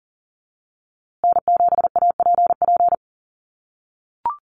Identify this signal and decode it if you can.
Morse code, heard as a keyed tone, about 40 wpm, 700 Hz. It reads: N7APP
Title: USA Call Signs Repeated 2x 40wpm